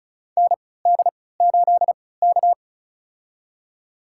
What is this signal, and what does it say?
Morse code, heard as a keyed tone, about 34 wpm, spaced as for 20 wpm, 700 Hz. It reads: ND8K
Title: USA Call Signs - 4 Characters - 3x Character Spacing - 35wpm